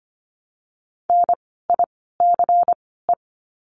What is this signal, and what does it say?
Morse code, heard as a keyed tone, about 25 wpm, spaced as for 16 wpm, 700 Hz. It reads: NICE
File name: course / 25:10wpm / 094 Lesson 16f - Letter c - Words 25wpm